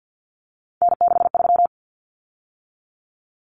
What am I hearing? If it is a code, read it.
Morse code, heard as a keyed tone, about 50 wpm, 700 Hz. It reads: N63
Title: Common Cut Numbers - 3-Digit 50wpm